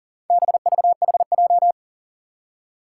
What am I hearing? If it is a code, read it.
Morse code, heard as a keyed tone, about 40 wpm, 700 Hz. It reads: BVHJ
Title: Four Challenging Letter-Number Combinations 40wpm